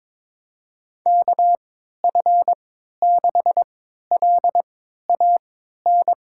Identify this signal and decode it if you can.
Morse code, heard as a keyed tone, about 22 wpm, spaced as for 13 wpm, 700 Hz. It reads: KF6LAN
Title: USA Call Signs - 6 Characters - 3x Character Spacing - 22wpm